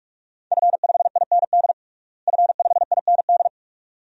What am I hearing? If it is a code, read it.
Morse code, heard as a keyed tone, about 46 wpm, 700 Hz. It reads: F5IND F5IND
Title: DXCC-100 Call Signs - 5 Characters - Repeated 2x - 45wpm